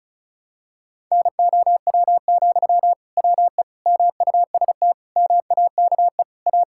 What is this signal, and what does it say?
Morse code, heard as a keyed tone, about 36 wpm, 700 Hz. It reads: NOW, WE MUST MAKE A
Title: Sentences from Top 1000 Words - Review Part 6 - 35wpm